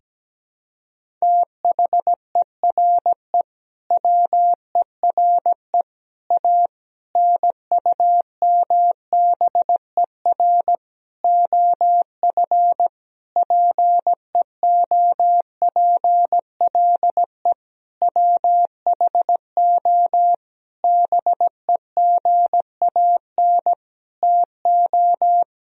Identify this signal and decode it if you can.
Morse code, heard as a keyed tone, about 17 wpm, 700 Hz. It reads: THERE WERE A NUMBER OF PEOPLE WHO BEGAN TO
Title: Sentences from Top 1000 Words - Review Part 1 - 17wpm